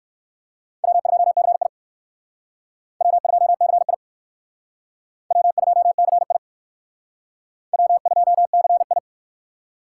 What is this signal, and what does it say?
Morse code, heard as a keyed tone, about 59 wpm, 700 Hz. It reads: W2CI W2CI W2CI W2CI